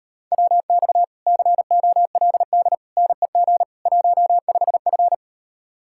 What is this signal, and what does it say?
Morse code, heard as a keyed tone, about 38 wpm, 700 Hz. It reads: WX COLD NEG 15F